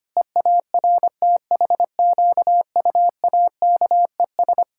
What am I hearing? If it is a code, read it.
Morse code, heard as a keyed tone, about 25 wpm, 700 Hz. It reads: EARTHQUAKES
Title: Pangram Challenge v2 - 25wpm